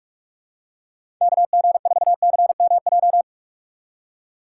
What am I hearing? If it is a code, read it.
Morse code, heard as a keyed tone, about 46 wpm, 690 Hz. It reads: KG4CMJ
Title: USA Call Signs - 6 Characters - 2x Character Spacing - 45wpm